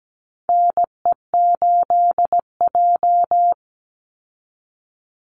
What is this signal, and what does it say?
Morse code, heard as a keyed tone, about 17 wpm, 700 Hz. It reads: NE8J
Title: Call Signs Repeated 3x - 17wpm